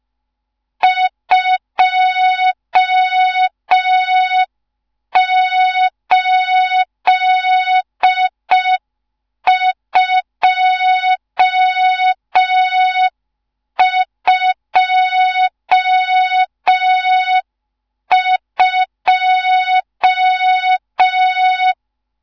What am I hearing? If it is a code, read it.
Morse code, heard as a keyed tone, about 5 wpm, 3700 Hz. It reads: 28222
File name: Morze-HA2MN / hang / 14 280 HA2MN